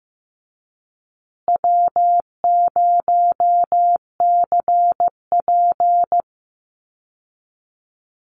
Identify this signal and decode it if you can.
Morse code, heard as a keyed tone, about 15 wpm, 700 Hz. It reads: W0CP
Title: Easy Call Signs Repeated 3x - 15wpm